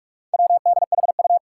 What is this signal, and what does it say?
Morse code, heard as a keyed tone, about 41 wpm, 700 Hz. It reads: WDHU
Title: Four Challenging Letter-Number Combinations 45wpm